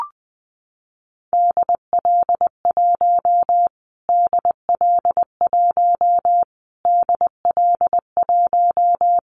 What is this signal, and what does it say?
Morse code, heard as a keyed tone, about 20 wpm, 700 Hz. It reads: DL1 DL1 DL1
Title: Callsign Prefixes for Top 25 DXCC - Repeated 3x - 20wpm